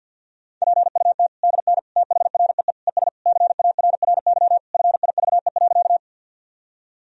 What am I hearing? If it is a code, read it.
Morse code, heard as a keyed tone, about 50 wpm, 700 Hz. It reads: PUT DN THREE ES CARRY FIVE.
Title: Sentences from Top 400 Words and Abbreviations 50wpm